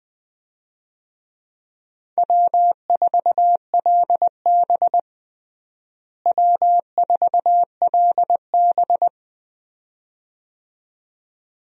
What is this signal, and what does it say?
Morse code, heard as a keyed tone, about 20 wpm, 700 Hz. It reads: W4LB W4LB
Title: USA Call Signs - 4 Characters - Repeated 2x - 20wpm